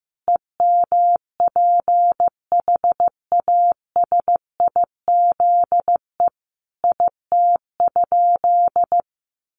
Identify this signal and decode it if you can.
Morse code, heard as a keyed tone, about 15 wpm, 700 Hz. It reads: EMPHASIZE IT?
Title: Sentences from Top 1500 Words 15wpm